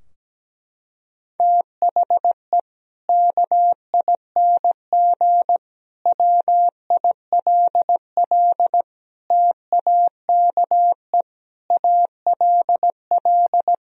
Morse code, heard as a keyed tone, about 17 wpm, 700 Hz. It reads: THE KING WILL TAKE ALL
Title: Sentences from Top 1000 Words - Review Part 4 - 17wpm